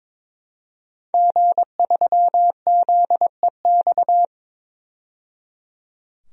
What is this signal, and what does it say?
Morse code, heard as a keyed tone, about 22 wpm, 700 Hz. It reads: G3ZEX
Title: DXCC-100 Call Signs - 5 Characters - 22wpm